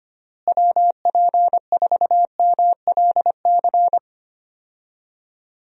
Morse code, heard as a keyed tone, about 25 wpm, 700 Hz. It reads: WP4MLC
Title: USA Call Signs - 6 Characters - Repeated 2x - 25wpm